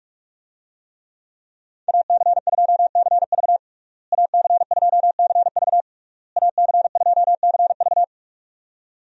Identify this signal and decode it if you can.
Morse code, heard as a keyed tone, about 46 wpm, 700 Hz. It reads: AC2CV AC2CV AC2CV
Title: USA Call Signs - 5 Characters - Repeated 3x - 45wpm